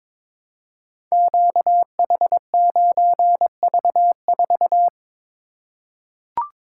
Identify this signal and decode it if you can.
Morse code, heard as a keyed tone, about 22 wpm, 690 Hz. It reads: QH9V4
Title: Code Groups - 5 Letter:Numbers - 22wpm